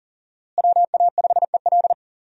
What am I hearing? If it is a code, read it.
Morse code, heard as a keyed tone, about 40 wpm, 700 Hz. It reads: WA5EL
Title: USA Call Signs - 5 Characters - Repeated 3x - 40wpm